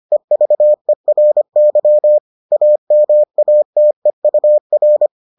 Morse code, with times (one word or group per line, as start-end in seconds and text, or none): 0.1-2.2 EVERY
2.5-5.1 AMATEUR